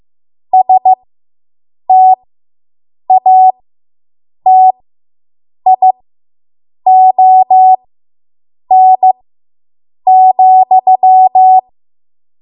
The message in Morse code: STATION,